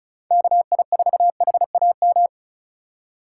Morse code, heard as a keyed tone, about 34 wpm, 700 Hz. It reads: KI4HAM
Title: USA Call Signs - 6 Characters - 2x Character Spacing - 35wpm